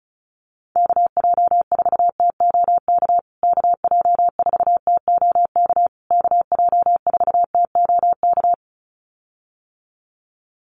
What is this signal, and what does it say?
Morse code, heard as a keyed tone, about 34 wpm, 700 Hz. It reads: KJ4TOK KJ4TOK KJ4TOK
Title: USA Call Signs Repeated 3x 35wpm